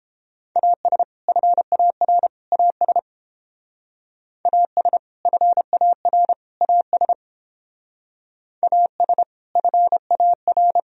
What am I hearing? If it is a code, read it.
Morse code, heard as a keyed tone, about 29 wpm, 700 Hz. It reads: AS FAR AS AS FAR AS AS FAR